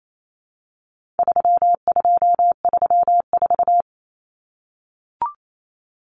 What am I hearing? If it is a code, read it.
Morse code, heard as a keyed tone, about 28 wpm, 700 Hz. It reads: 3234
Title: Common Cut Numbers - 4-Digit 28wpm